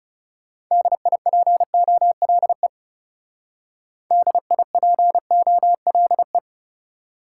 Morse code, heard as a keyed tone, about 38 wpm, 700 Hz. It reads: DIPOLE DIPOLE